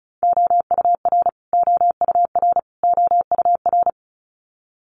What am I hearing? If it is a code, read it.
Morse code, heard as a keyed tone, about 34 wpm, 700 Hz. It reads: OUR OUR OUR
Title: Top 500 Words Repeated 35wpm